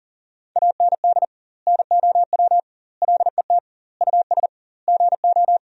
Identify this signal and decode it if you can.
Morse code, heard as a keyed tone, about 40 wpm, 700 Hz. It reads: AND NOW LET US GO
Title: 2x Word-Spacing - Sentences from Top 300 Words - 40wpm